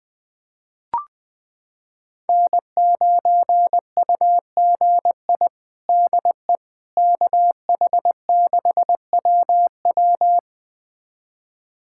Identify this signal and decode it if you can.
Morse code, heard as a keyed tone, about 21 wpm, 700 Hz. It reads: N9UGI DE KH6WW
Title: USA Callsign DE Callsign - 5 Characters - 20wpm